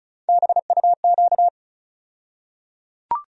Morse code, heard as a keyed tone, about 36 wpm, 700 Hz. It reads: BUQ
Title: USA Call Sign Postfixes - 3 Letters 35wpm